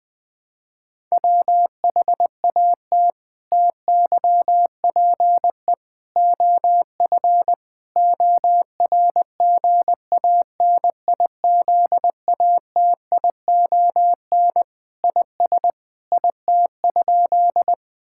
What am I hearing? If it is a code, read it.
Morse code, heard as a keyed tone, about 20 wpm, 700 Hz. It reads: WHAT TYPE OF ORGANIZATION IS IT?
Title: Sentences from Top 1000 Words - Review Part 5 - 20wpm